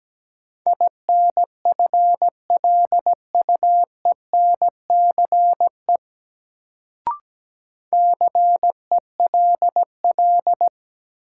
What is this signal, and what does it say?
Morse code, heard as a keyed tone, about 17 wpm, 700 Hz. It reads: INFLUENCE CELL
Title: Sentences from Top 1100 Words - New Word Drill - 17wpm